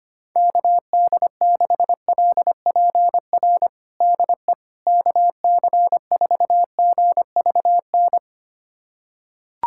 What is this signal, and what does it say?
Morse code, heard as a keyed tone, about 25 wpm, 700 Hz. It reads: KD6LPR DE KC4GVN
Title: USA Callsign DE Callsign - 6 Characters - 25wpm